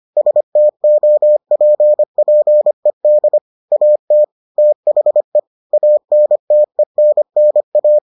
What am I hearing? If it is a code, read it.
Morse code, heard as a keyed tone, about 25 wpm, 600 Hz. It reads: STOPPED AT THE ANTENNA